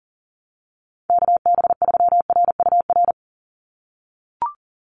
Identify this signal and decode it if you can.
Morse code, heard as a keyed tone, about 38 wpm, 700 Hz. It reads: KB3RUR
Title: USA Call Signs 40wpm